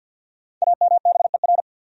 Morse code, heard as a keyed tone, about 52 wpm, 700 Hz. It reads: AMBER
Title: USA Common Names - Max 5 Letters - 50wpm